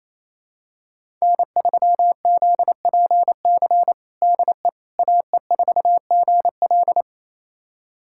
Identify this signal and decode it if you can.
Morse code, heard as a keyed tone, about 28 wpm, 700 Hz. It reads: N3ZPC DE AE4GL
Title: USA Callsign DE Callsign - 5 Characters - 28wpm